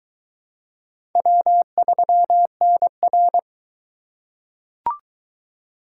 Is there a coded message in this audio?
Yes: Morse code from 1.2 to 3.4 s, heard as a keyed tone, about 20 wpm, 700 Hz. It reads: W3NR